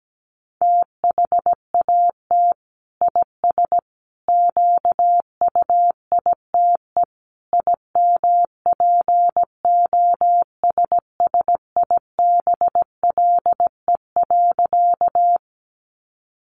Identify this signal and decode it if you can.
Morse code, heard as a keyed tone, about 17 wpm, 700 Hz. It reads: THAT IS QUITE IMPOSSIBLE.
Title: Sentences from Top 1100 Words 17wpm